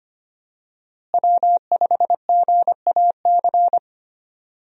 Morse code, heard as a keyed tone, about 25 wpm, 700 Hz. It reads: W5GAC
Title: USA Call Signs - 5 Characters - Repeated 2x - 25wpm